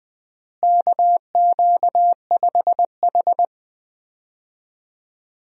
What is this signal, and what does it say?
Morse code, heard as a keyed tone, about 21 wpm, 700 Hz. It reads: KQ5H
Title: USA Call Signs - 4 Characters - Repeated 3x - 20wpm